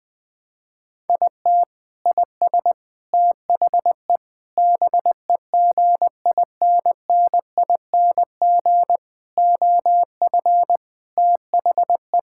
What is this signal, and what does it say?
Morse code, heard as a keyed tone, about 21 wpm, 700 Hz. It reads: IT IS THE BEGINNING OF THE
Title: Sentences from Top 1000 Words - Review Part 5 - 20wpm